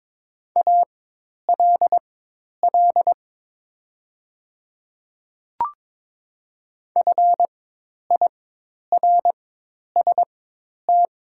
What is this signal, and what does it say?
Morse code, heard as a keyed tone, about 22 wpm, spaced as for 10 wpm, 700 Hz. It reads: ALL FIRST
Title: 4x Character Spacing - Top 300 Words - 22wpm